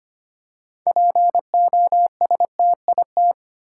Morse code, heard as a keyed tone, about 25 wpm, 700 Hz. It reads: POSTIT
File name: practice / 25wpm / Difficult Words with Hints - 25wpm